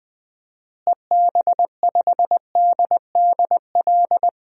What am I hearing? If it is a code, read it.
Morse code, heard as a keyed tone, about 20 wpm, 700 Hz. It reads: EB5DDL
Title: DXCC-100 Call Signs - 6 Characters - Repeated 2x - 20wpm